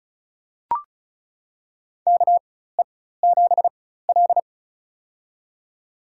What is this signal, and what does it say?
Morse code, heard as a keyed tone, about 34 wpm, spaced as for 16 wpm, 700 Hz. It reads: KE7L
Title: USA Call Signs - 4x Character Spacing - 35wpm